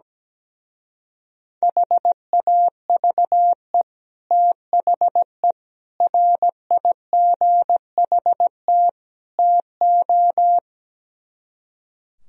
Morse code, heard as a keyed tone, about 17 wpm, 700 Hz. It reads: HAVE THE RIGHT TO
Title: Speed-Racing - Sets of 4 Words 17wpm